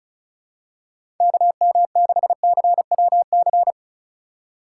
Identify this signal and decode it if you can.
Morse code, heard as a keyed tone, about 36 wpm, 700 Hz. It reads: KM6CWC
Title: USA Call Signs - 6 Characters - 4x Character Spacing - 35wpm